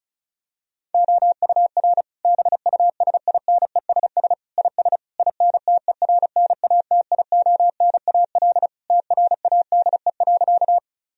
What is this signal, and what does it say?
Morse code, heard as a keyed tone, about 36 wpm, 700 Hz. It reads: OUR BUSINESS IS INTERNATIONAL TRADE.
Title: Sentences from Top 1000 Words - Review Part 4 - 35wpm